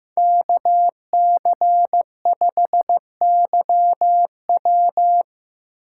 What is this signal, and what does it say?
Morse code, heard as a keyed tone, about 20 wpm, spaced as for 18 wpm, 700 Hz. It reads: KC5YW
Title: Speed Racing - USA Call Signs 15wpm